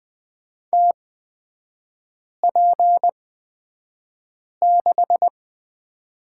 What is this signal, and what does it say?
Morse code, heard as a keyed tone, about 20 wpm, spaced as for 5 wpm, 700 Hz. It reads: TP6
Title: 224 Lesson 36e - BT - Three Characters 20wpm